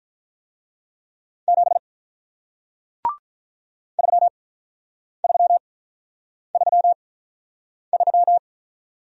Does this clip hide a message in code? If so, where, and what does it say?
Yes: Morse code from 1.5 to 8.4 s, heard as a keyed tone, about 48 wpm, spaced as for 9 wpm, 700 Hz. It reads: 7 3333